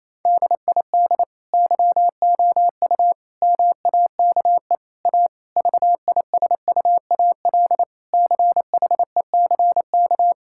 Morse code, heard as a keyed tone, about 28 wpm, 700 Hz. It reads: DID YOU MAKE A VISUAL CHECK
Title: Sentences from Top 1000 Words - Review Part 6 - 28wpm